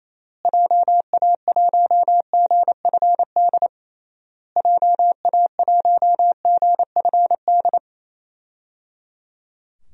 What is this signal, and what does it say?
Morse code, heard as a keyed tone, about 28 wpm, 700 Hz. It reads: JA1GFD JA1GFD
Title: DXCC-100 Call Signs - 6 Characters - Repeated 3x - 28wpm